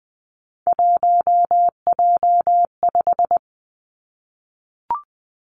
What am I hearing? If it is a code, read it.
Morse code, heard as a keyed tone, about 20 wpm, 700 Hz. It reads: 1J5
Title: Three Challenging Letter-Number Combinations 20wpm